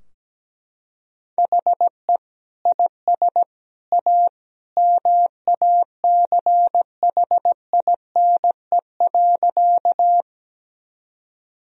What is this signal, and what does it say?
Morse code, heard as a keyed tone, about 17 wpm, 700 Hz. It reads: HE IS A MACHINE.
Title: Sentences from Top 1000 Words - Review Part 2 - 17wpm